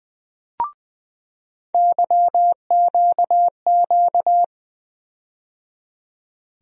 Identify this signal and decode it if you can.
Morse code, heard as a keyed tone, about 20 wpm, 700 Hz. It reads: YQQ
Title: USA Call Sign Postfixes - 3 Letters 20wpm